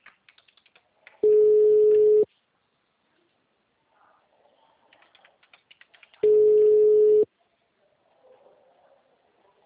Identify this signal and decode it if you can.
Ringback tone (Europe, 425 Hz).